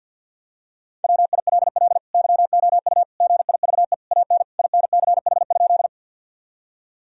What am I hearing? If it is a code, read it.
Morse code, heard as a keyed tone, about 52 wpm, 700 Hz. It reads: WILL YOU GIVE AN INCH?
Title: Sentences from Top 1000 Words - Review Part 3 - 50wpm